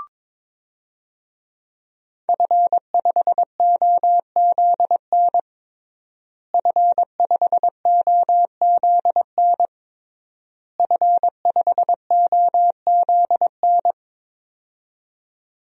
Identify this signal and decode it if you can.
Morse code, heard as a keyed tone, about 22 wpm, 700 Hz. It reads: F5OZN F5OZN F5OZN
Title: DXCC-100 Call Signs - 5 Characters - Repeated 3x - 22wpm